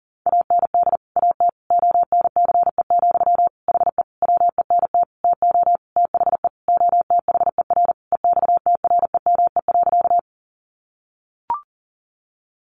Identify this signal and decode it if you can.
Morse code, heard as a keyed tone, about 37 wpm, 700 Hz. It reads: AND AT ONCE, HE WENT TO THE OTHER EXTREME.
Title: Sentences from Top 1300 Words 40wpm